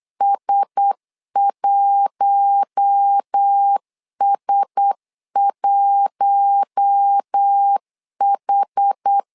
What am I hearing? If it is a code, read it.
Morse code, heard as a keyed tone, about 8 wpm, 800 Hz. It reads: S1S1H
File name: LEZIONI / 4 Q1S 8wpm 2710